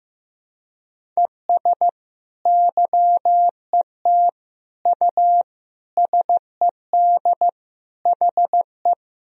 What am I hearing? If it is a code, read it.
Morse code, heard as a keyed tone, about 15 wpm, 700 Hz. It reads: ES YET U SED HE
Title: Sentences from Top 500 Words and Abbreviations 15wpm